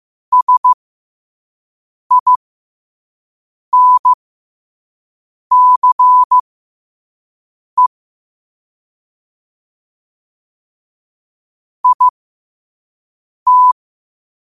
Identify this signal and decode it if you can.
Morse code, heard as a keyed tone, about 15 wpm, spaced as for 5 wpm, 1000 Hz. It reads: SINCE IT
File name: qotdfiles / 5wpm